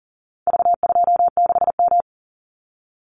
Morse code, heard as a keyed tone, about 40 wpm, 700 Hz. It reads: V26M
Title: Call Signs Repeated 2x - 40wpm